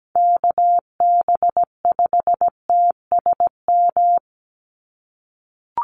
Morse code, heard as a keyed tone, about 17 wpm, 700 Hz. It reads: KB5TSM
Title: USA Call Signs 17wpm